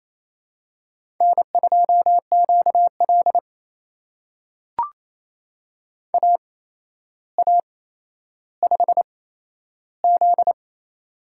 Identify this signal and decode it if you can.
Morse code, heard as a keyed tone, about 28 wpm, spaced as for 12 wpm, 700 Hz. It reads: N2QL AA5Z
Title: USA Call Signs - 4 Characters - 8x Character Spacing - 28wpm